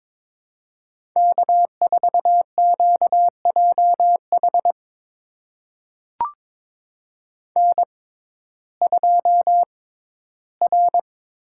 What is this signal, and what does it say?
Morse code, heard as a keyed tone, about 22 wpm, spaced as for 13 wpm, 700 Hz. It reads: K4QJH N2R
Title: USA Call Signs - 5 Characters - 6x Character Spacing - 22wpm